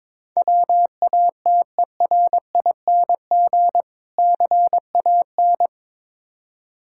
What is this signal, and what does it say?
Morse code, heard as a keyed tone, about 22 wpm, 700 Hz. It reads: WATERING CAN
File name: practice / 22wpm / Difficult Words with Hints - 22wpm